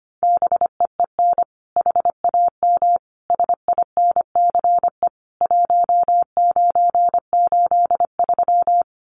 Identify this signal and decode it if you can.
Morse code, heard as a keyed tone, about 25 wpm, 700 Hz. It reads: BEEN HAM SINCE 1983
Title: Common QSO Phrases 25wpm